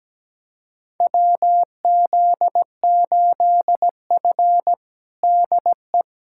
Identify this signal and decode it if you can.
Morse code, heard as a keyed tone, about 17 wpm, 700 Hz. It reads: WZ8F DE
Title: USA Callsign DE Callsign - 4 Characters - 17wpm